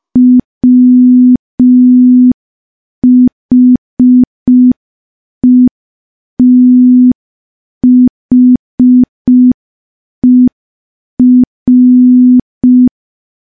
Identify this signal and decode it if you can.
Morse code, heard as a keyed tone, about 5 wpm, 250 Hz. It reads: WHETHER
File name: morse / morse-training-05WPM